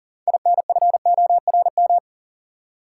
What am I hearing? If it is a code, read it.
Morse code, heard as a keyed tone, about 40 wpm, 700 Hz. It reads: INFORM
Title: Six Letter Words - 40wpm